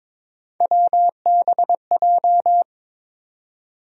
Morse code, heard as a keyed tone, about 22 wpm, 700 Hz. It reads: WBJ
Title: USA Call Sign Postfixes - 3 Letters 22wpm